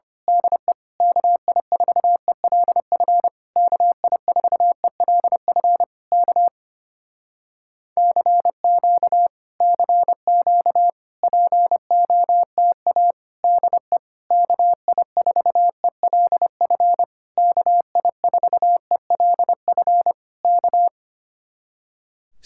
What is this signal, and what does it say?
Morse code, heard as a keyed tone, about 25 wpm, 700 Hz. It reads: DE KI4ELF KI4ELF K CQ CQ POTA DE KI4ELF KI4ELF K
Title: Speed-Racing Common QSO Phrases - 25wpm